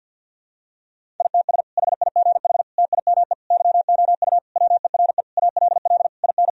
Morse code, heard as a keyed tone, about 51 wpm, 700 Hz. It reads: ITS HIGH TIME YOU WERE ALL IN
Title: Sentences from Top 1000 Words - Review Part 1 - 50wpm